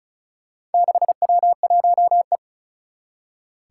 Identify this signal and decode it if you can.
Morse code, heard as a keyed tone, about 36 wpm, 690 Hz. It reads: 6W1E
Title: Four Challenging Letter-Number Combinations 35wpm